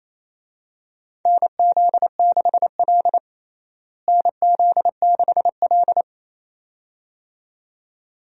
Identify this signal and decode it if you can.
Morse code, heard as a keyed tone, about 28 wpm, 700 Hz. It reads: NZ6L NZ6L
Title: USA Call Signs - 4 Characters - Repeated 2x - 28wpm